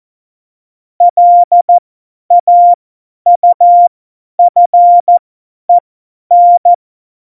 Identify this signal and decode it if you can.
Morse code, heard as a keyed tone, about 14 wpm, spaced as for 10 wpm, 700 Hz. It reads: LAUFEN
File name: Morsekurs / Witze / Witz 11. Dezember, 50 F70